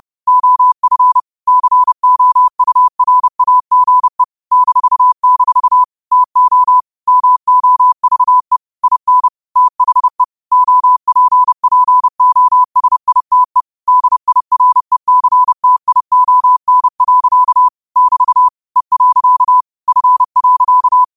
Morse code, heard as a keyed tone, about 30 wpm, 1000 Hz. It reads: OF COURAGE -- TO MOVE IN THE OPPOSITE DIRECTION. <BT> E. F.